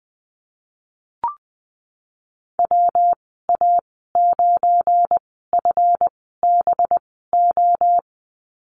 Morse code, heard as a keyed tone, about 20 wpm, spaced as for 14 wpm, 700 Hz. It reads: WA9FBO
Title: 186 Lesson 30g - Letter b - Call Signs 20wpm